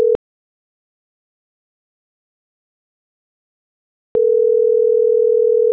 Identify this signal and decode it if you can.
Ringback tone (North America), 440 + 480 Hz.
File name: CENTRAL MAERSK / out sandovalmsk